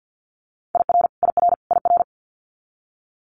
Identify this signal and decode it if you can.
Morse code, heard as a keyed tone, about 50 wpm, 700 Hz. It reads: IR IR IR
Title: Multi-letter Phonemes Repeated 3x 50wpm